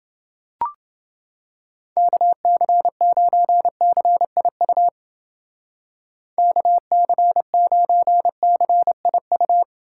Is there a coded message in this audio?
Yes: Morse code from 2.0 to 9.6 s, heard as a keyed tone, about 30 wpm, 700 Hz. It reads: KC9CIU KC9CIU